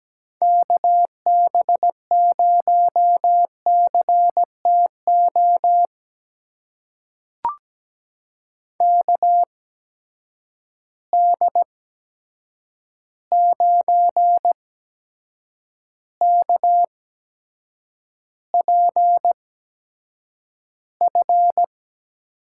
Morse code, heard as a keyed tone, about 17 wpm, spaced as for 7 wpm, 700 Hz. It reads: KB0CTO KD9KPF
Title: USA Call Signs - 6 Characters - 8x Character Spacing - 17wpm